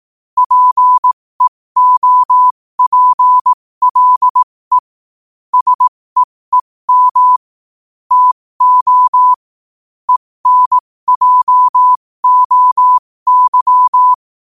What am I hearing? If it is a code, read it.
Morse code, heard as a keyed tone, about 18 wpm, spaced as for 15 wpm, 1000 Hz. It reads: PEOPLE SEEM TO ENJOY
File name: qotdfiles / 15wpm